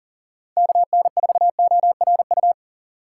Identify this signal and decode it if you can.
Morse code, heard as a keyed tone, about 39 wpm, 700 Hz. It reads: KN4ORU
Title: USA Call Signs - 4x Character Spacing - 40wpm